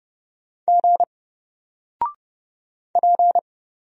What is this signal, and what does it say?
Morse code, heard as a keyed tone, about 30 wpm, 700 Hz. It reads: G P
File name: Single-Letter-Number-30wpm